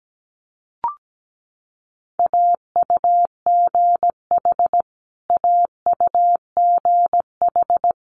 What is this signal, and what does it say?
Morse code, heard as a keyed tone, about 17 wpm, 700 Hz. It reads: AUGH AUGH
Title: Multi-letter Phonemes Repeated 3x 17wpm